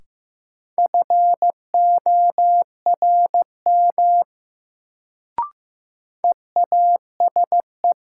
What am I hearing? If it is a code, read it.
Morse code, heard as a keyed tone, about 15 wpm, 700 Hz. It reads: FORM EASE
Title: Sentences from Top 1000 Words - Review Part 1 - New Word Drill - 15wpm